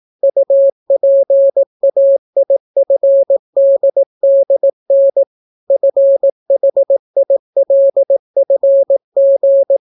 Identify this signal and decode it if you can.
Morse code, heard as a keyed tone, about 18 wpm, 550 Hz. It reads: UPAIFDDN FHILFG